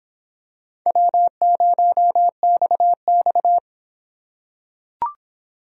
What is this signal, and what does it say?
Morse code, heard as a keyed tone, about 24 wpm, 700 Hz. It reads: W0XX